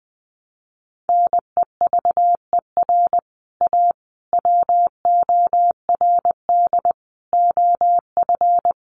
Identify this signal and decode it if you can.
Morse code, heard as a keyed tone, about 20 wpm, 700 Hz. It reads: NEVER A WORD OF
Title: Sentences from Top 1300 Words 20wpm